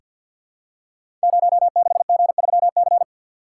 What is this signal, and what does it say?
Morse code, heard as a keyed tone, about 49 wpm, 700 Hz. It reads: 06G3C
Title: Code Groups - 5 Letter:Numbers - 50wpm